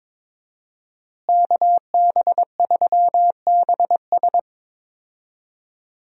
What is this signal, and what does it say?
Morse code, heard as a keyed tone, about 22 wpm, 700 Hz. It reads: KB3BS
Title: USA Call Signs - 5 Characters - Repeated 3x - 22wpm